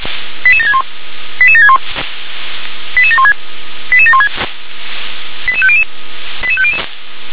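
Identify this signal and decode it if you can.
Five-tone selective call (ZVEI), heard as quick runs of five tones, tones of about 70 ms. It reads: 88761 88761 88716 88716 88500 88500
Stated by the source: Received on an SDR, ZVEI five-tone selective call.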